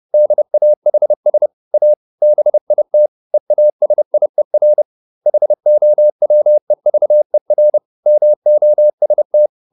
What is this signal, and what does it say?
Morse code, heard as a keyed tone, about 30 wpm, 600 Hz. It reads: DAHS A BIT EASIER HOWEVER MOST